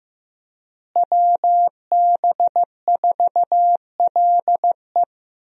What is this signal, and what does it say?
Morse code, heard as a keyed tone, about 15 wpm, 700 Hz. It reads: WB4LE
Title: USA Call Signs - 5 Characters - Repeated 2x - 15wpm